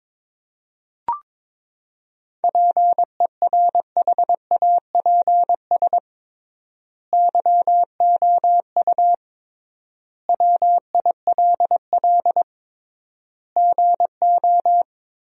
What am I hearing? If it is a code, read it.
Morse code, heard as a keyed tone, about 22 wpm, 700 Hz. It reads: PERHAPS YOU WILL GO
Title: 3x Word-Spacing - Sentences from Top 500 Words - 22wpm